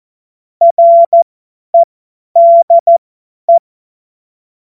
Morse code, heard as a keyed tone, about 14 wpm, spaced as for 10 wpm, 680 Hz. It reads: REDE
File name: Morsekurs / Witze / Witz 14. Februar 2023 50 F70